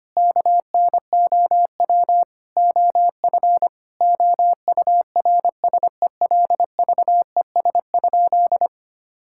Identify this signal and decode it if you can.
Morse code, heard as a keyed tone, about 25 wpm, 700 Hz. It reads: KNOW OF OURSELVES?
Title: Sentences from Top 1600 Words 25wpm